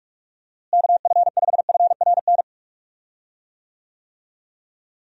Morse code, heard as a keyed tone, about 42 wpm, 700 Hz. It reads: KF5FRN
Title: USA Call Signs - 6 Characters - Repeated 2x - 45wpm